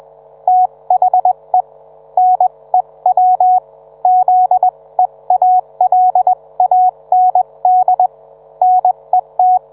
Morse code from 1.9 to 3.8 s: NEW